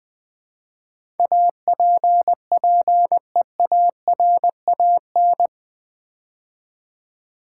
Morse code, heard as a keyed tone, about 20 wpm, 700 Hz. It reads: APPEARAN
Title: Build A Word - 4 Letters At A Time - 10-Letter Words - 20wpm